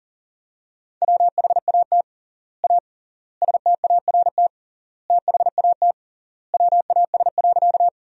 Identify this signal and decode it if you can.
Morse code, heard as a keyed tone, about 37 wpm, 700 Hz. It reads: WHAT A START THAT WAS.
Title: 3x Word-Spacing - Sentences from Top 300 Words - 40wpm